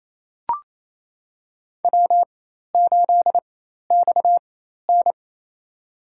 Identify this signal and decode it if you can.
Morse code, heard as a keyed tone, about 28 wpm, spaced as for 13 wpm, 700 Hz. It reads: W8XN
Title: USA Call Signs - 4 Characters - 4x Character Spacing - 28wpm